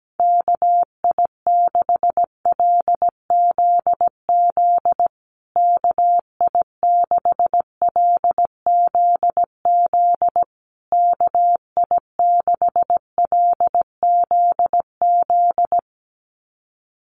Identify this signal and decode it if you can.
Morse code, heard as a keyed tone, about 17 wpm, 700 Hz. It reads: KI6LZZ KI6LZZ KI6LZZ
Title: USA Call Signs Repeated 3x 17wpm